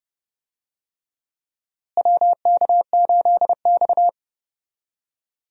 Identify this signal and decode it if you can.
Morse code, heard as a keyed tone, about 32 wpm, 700 Hz. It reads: WK8X
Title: USA Call Signs - 4 Characters - Repeated 3x - 30wpm